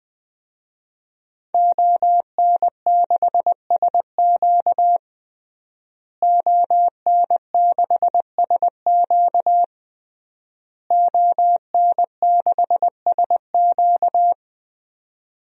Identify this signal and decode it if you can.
Morse code, heard as a keyed tone, about 21 wpm, 700 Hz. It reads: ON6SQ ON6SQ ON6SQ
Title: DXCC-100 Call Signs - 5 Characters - Repeated 3x - 20wpm